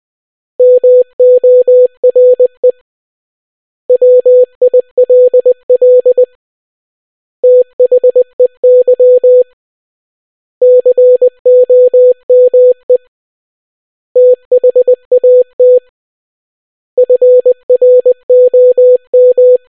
Morse code, heard as a keyed tone, about 20 wpm, 500 Hz. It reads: MORE WILL THEY COME THAT FROM